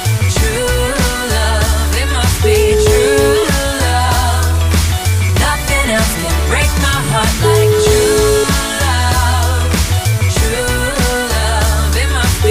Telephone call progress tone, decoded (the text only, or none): ringback tone